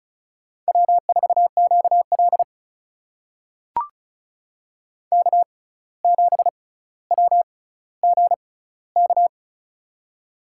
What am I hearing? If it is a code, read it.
Morse code, heard as a keyed tone, about 34 wpm, spaced as for 17 wpm, 700 Hz. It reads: W4QL K7WGK
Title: USA Call Signs - 6x Character Spacing - 35wpm